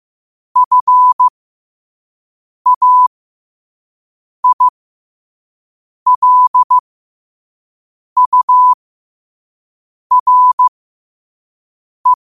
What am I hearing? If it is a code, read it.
Morse code, heard as a keyed tone, about 15 wpm, spaced as for 5 wpm, 1000 Hz. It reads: FAILURE
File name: qotdfiles / 5wpm